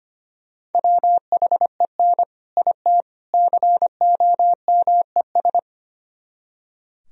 Morse code, heard as a keyed tone, about 31 wpm, 700 Hz. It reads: WHEN IT COMES
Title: Speed-Racing - Sets of 3 Words 25wpm